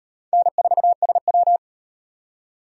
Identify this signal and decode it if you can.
Morse code, heard as a keyed tone, about 38 wpm, 700 Hz. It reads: N4SW